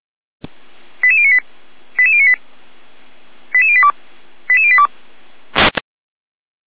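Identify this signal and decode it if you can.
Five-tone selective call (ZVEI), heard as quick runs of five tones, tones of about 70 ms. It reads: 88098 88098 88082 88082
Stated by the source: ZVEI five-tone selective call received by SDR.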